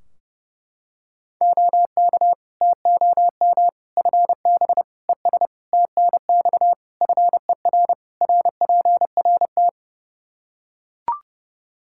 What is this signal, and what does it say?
Morse code, heard as a keyed tone, about 30 wpm, 700 Hz. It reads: OK TOM FB ES TNX FER RPRT
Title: Practice QSO - Exchange 2 - Info - 30wpm